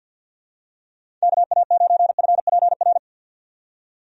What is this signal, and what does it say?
Morse code, heard as a keyed tone, about 52 wpm, 700 Hz. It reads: KA9FPR
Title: USA Call Signs - 6 Characters - 50wpm